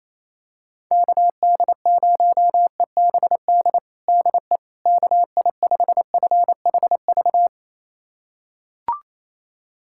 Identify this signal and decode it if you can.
Morse code, heard as a keyed tone, about 28 wpm, 700 Hz. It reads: KD0EBD DE KI5FHV
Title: USA Callsign DE Callsign - 6 Characters - 28wpm